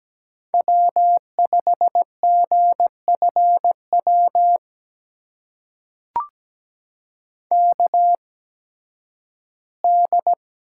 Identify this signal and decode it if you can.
Morse code, heard as a keyed tone, about 17 wpm, spaced as for 11 wpm, 700 Hz. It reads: W5GFW KD
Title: USA Call Signs - 5 Characters - 8x Character Spacing - 17wpm